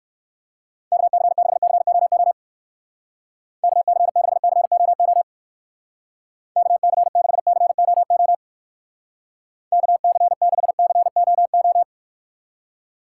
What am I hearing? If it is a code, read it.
Morse code, heard as a keyed tone, about 62 wpm, 700 Hz. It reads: KC6COO KC6COO KC6COO KC6COO